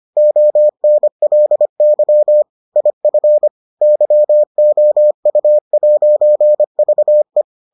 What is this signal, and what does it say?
Morse code, heard as a keyed tone, about 25 wpm, 600 Hz. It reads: ONLY IF YOU'VE